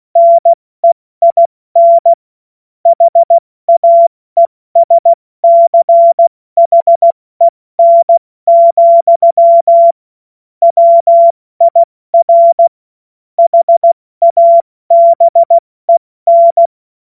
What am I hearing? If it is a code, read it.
Morse code, heard as a keyed tone, about 16 wpm, 680 Hz. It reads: NEIN HAESCHEN, WIR HABEN
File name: Morsekurs / Witze / Witz 25. Mai 2021, 70 F80